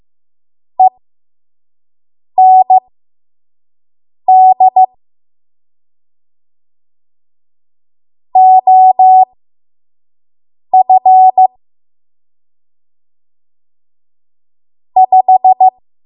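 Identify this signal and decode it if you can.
Morse code, heard as a keyed tone, about 15 wpm, spaced as for 5 wpm, 750 Hz. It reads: END OF 5
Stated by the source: 5 WPM Morse code practice.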